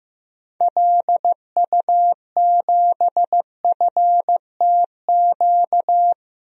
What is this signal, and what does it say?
Morse code, heard as a keyed tone, about 15 wpm, 700 Hz. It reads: LU7FTQ
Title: DXCC-100 Call Signs - 6 Characters - 15wpm